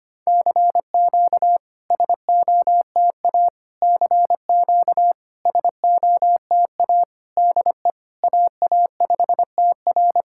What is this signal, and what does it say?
Morse code, heard as a keyed tone, about 25 wpm, 690 Hz. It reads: CQ SOTA CQ SOTA DE AA5TR
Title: SOTA QSO Practice 25wpm